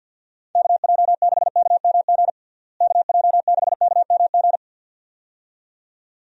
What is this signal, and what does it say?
Morse code, heard as a keyed tone, about 46 wpm, 700 Hz. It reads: KJ6KMG KJ6KMG
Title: USA Call Signs - 6 Characters - Repeated 2x - 50wpm